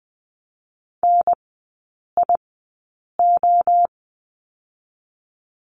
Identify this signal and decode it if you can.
Morse code, heard as a keyed tone, about 20 wpm, 700 Hz. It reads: N I O
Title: 029 Lesson 6e - Letter i - Three Characters 20wpm